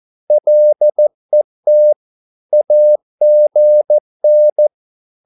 Morse code, heard as a keyed tone, about 14 wpm, 600 Hz. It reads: LET AGN
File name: Korte ord DK 14wpm